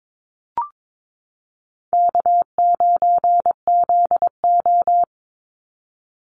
Morse code, heard as a keyed tone, about 22 wpm, 700 Hz. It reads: K9ZO
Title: Easy Call Signs - 22wpm